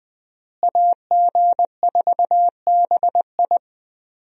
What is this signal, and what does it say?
Morse code, heard as a keyed tone, about 20 wpm, 700 Hz. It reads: AG4BI
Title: USA Call Signs - 5 Characters - Repeated 3x - 20wpm